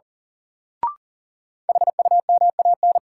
Morse code, heard as a keyed tone, about 37 wpm, 700 Hz. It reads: HUMAN
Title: Sets of 5 Words - New Word Drill - 40wpm